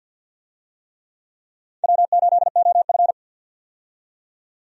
Morse code, heard as a keyed tone, about 49 wpm, 700 Hz. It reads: W8OF
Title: USA Call Signs - 4 Characters - 50wpm